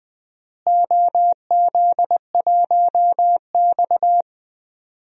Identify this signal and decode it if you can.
Morse code, heard as a keyed tone, about 20 wpm, 700 Hz. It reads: OZ1X
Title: DXCC-100 Call Signs - 4 Characters - Repeated 2x - 20wpm